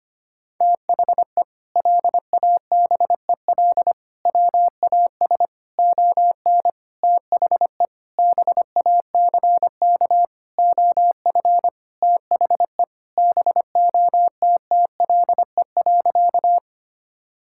Morse code, heard as a keyed tone, about 25 wpm, 700 Hz. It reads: THE LABEL WAS ON THE BACK OF THE BOTTLE.
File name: practice / 25wpm / Sentences with 4-Word Phrases - 25wpm